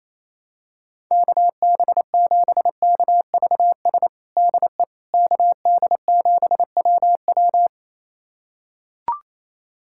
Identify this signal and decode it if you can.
Morse code, heard as a keyed tone, about 28 wpm, 700 Hz. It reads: KB7KVS DE KD7WW
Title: USA Callsign DE Callsign 28wpm